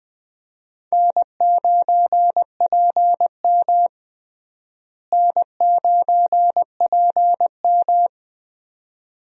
Morse code, heard as a keyed tone, about 20 wpm, 700 Hz. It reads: N9PM N9PM
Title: USA Call Signs - 4 Characters - Repeated 2x - 20wpm